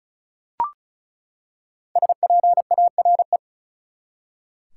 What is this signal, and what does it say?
Morse code, heard as a keyed tone, about 36 wpm, 700 Hz. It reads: SPARE
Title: Five Letter Words - 35wpm